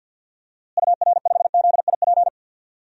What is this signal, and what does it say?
Morse code, heard as a keyed tone, about 48 wpm, 700 Hz. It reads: UR5ZIP
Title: DXCC-100 Call Signs - 6 Characters - Repeated 3x - 50wpm